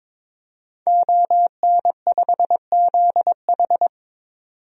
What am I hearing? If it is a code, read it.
Morse code, heard as a keyed tone, about 22 wpm, 700 Hz. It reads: ON5ZH
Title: DXCC-100 Call Signs - 5 Characters - 22wpm